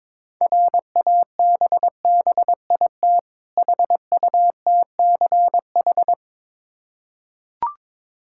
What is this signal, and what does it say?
Morse code, heard as a keyed tone, about 22 wpm, 690 Hz. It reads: RABBIT HUTCH
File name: Difficult Words with Hints - 22wpm